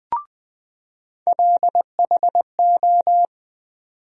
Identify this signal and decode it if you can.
Morse code, heard as a keyed tone, about 20 wpm, 700 Hz. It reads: LHO
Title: USA Call Sign Postfixes - 3 Letters 20wpm